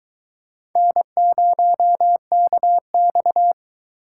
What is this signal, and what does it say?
Morse code, heard as a keyed tone, about 23 wpm, 700 Hz. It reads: N0KX